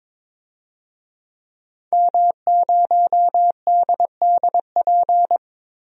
Morse code, heard as a keyed tone, about 22 wpm, 700 Hz. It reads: M0DDP
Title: DXCC-100 Call Signs - 5 Characters - 22wpm